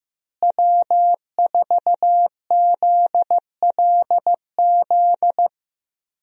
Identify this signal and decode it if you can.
Morse code, heard as a keyed tone, about 15 wpm, 700 Hz. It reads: W4ZLZ
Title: USA Call Signs - 5 Characters - Repeated 3x - 15wpm